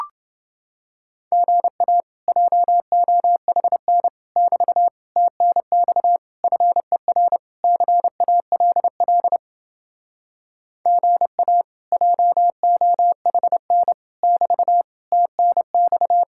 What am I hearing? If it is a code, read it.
Morse code, heard as a keyed tone, about 29 wpm, 700 Hz. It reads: GA JOHN <BT> TNX FER CALL GA JOHN <BT> TNX